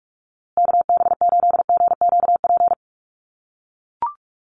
Morse code, heard as a keyed tone, about 44 wpm, 690 Hz. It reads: KB8GQP
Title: USA Call Signs Repeated 3x 45wpm